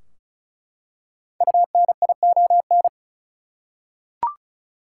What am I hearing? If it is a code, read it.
Morse code, heard as a keyed tone, about 36 wpm, 700 Hz. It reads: UNION
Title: Five Letter Words - 35wpm